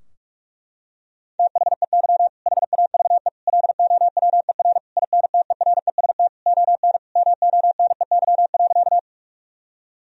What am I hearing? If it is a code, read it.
Morse code, heard as a keyed tone, about 46 wpm, 700 Hz. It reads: THEY HAVE LOWER INTEREST ON MONEY.
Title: Sentences from Top 1000 Words - Review Part 4 - 45wpm